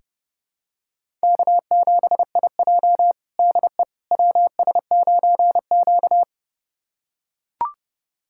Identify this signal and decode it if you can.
Morse code, heard as a keyed tone, about 30 wpm, 700 Hz. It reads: K7IJ DE WS9Q
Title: USA Callsign DE Callsign - 4 Characters - 30wpm